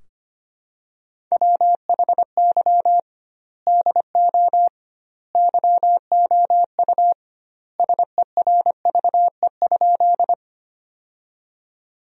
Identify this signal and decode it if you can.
Morse code, heard as a keyed tone, about 25 wpm, 700 Hz. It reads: WHY DO YOU SERVE?
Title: 2x Word-Spacing - Sentences from Top 400 Words - 25wpm